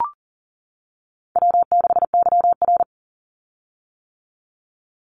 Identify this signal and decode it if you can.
Morse code, heard as a keyed tone, about 40 wpm, 700 Hz. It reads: W6YR
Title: Easy Call Signs - 40wpm